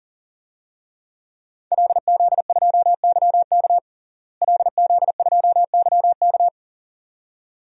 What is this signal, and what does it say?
Morse code, heard as a keyed tone, about 40 wpm, 700 Hz. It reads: LZ2YK LZ2YK
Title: DXCC-100 Call Signs - 5 Characters - Repeated 2x - 40wpm